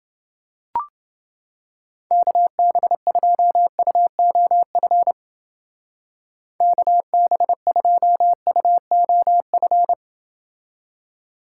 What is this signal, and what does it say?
Morse code, heard as a keyed tone, about 29 wpm, 700 Hz. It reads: KB2UOF KB2UOF